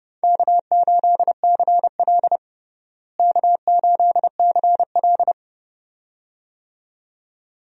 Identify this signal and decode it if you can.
Morse code, heard as a keyed tone, about 32 wpm, 700 Hz. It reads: K8CL K8CL
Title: USA Call Signs - 4 Characters - Repeated 2x - 30wpm